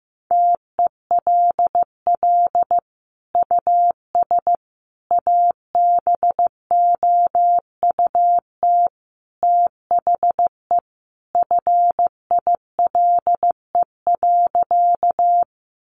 Morse code, heard as a keyed tone, about 15 wpm, 700 Hz. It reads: TELL US ABOUT THE FILE.
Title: Sentences from Top 1200 Words 15wpm